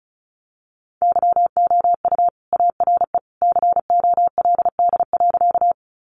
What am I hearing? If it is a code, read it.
Morse code, heard as a keyed tone, about 34 wpm, 700 Hz. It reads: YOU ARE COLD.
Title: Sentences from Top 400 Words 35wpm